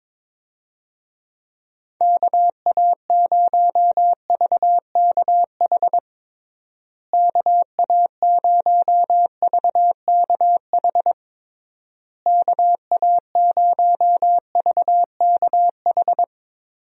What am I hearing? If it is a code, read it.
Morse code, heard as a keyed tone, about 22 wpm, 700 Hz. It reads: KA0VKH KA0VKH KA0VKH
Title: USA Call Signs - 6 Characters - Repeated 3x - 22wpm